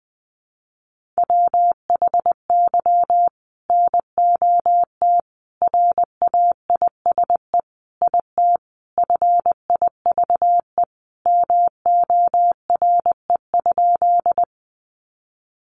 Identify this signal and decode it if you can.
Morse code, heard as a keyed tone, about 20 wpm, 700 Hz. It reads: WHY NOT RAISE IT FIVE MORE?
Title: Sentences from Top 1400 Words 20wpm